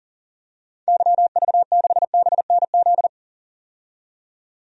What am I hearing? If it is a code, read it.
Morse code, heard as a keyed tone, about 40 wpm, 700 Hz. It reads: YV6BNZ
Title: DXCC-100 Call Signs - 6 Characters - 40wpm